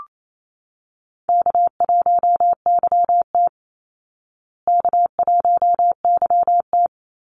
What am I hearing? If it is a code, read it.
Morse code, heard as a keyed tone, about 28 wpm, 700 Hz. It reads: K1YT K1YT
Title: Easy Call Signs Repeated 3x - 28wpm